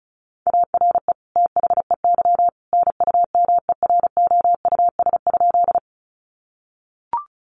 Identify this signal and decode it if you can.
Morse code, heard as a keyed tone, about 34 wpm, 700 Hz. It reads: ARE THEY NUMEROUS?
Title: Sentences from Top 1400 Words 35wpm